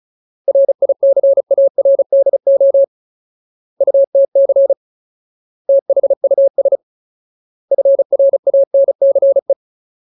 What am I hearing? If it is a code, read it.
Morse code, heard as a keyed tone, about 34 wpm, 550 Hz. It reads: RICARDO UTC THUS FRANCE
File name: Ultimate Word List / 35WPM Ultimate